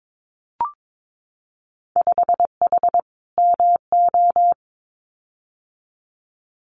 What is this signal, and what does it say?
Morse code, heard as a keyed tone, about 22 wpm, 700 Hz. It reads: 5H MO
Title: Winter Field Day Exchanges Encore 22wpm